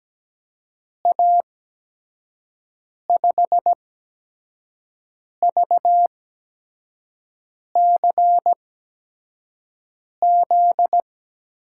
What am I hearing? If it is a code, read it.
Morse code, heard as a keyed tone, about 17 wpm, spaced as for 5 wpm, 700 Hz. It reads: A5VCZ
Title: USA Call Signs - 6 Characters - 8x Character Spacing - 17wpm